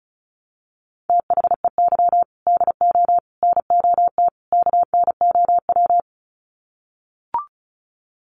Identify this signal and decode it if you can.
Morse code, heard as a keyed tone, about 34 wpm, 700 Hz. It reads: THEY DO NOT KNOW
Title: Sets of 4 Words 35wpm